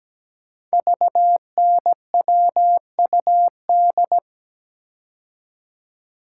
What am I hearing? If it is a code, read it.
Morse code, heard as a keyed tone, about 17 wpm, 700 Hz. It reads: VNWUD
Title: Code Groups - 5 Letters - 17wpm